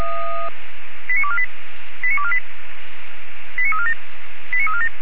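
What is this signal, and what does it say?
Five-tone selective call (ZVEI), heard as quick runs of five tones, tones of about 70 ms. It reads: 89248 89248 89347 89347
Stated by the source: ZVEI five-tone selective call received with an SDR.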